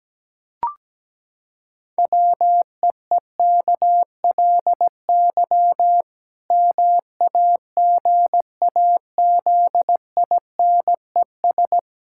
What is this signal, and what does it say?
Morse code, heard as a keyed tone, about 17 wpm, 700 Hz. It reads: WEEKLY MAGAZINES
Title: Pangram Challenge v2 - 17wpm